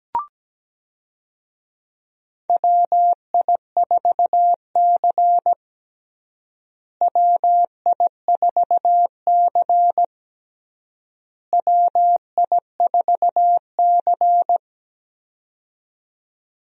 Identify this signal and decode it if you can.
Morse code, heard as a keyed tone, about 17 wpm, 700 Hz. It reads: WI4C WI4C WI4C
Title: USA Call Signs - 4 Characters - Repeated 3x - 17wpm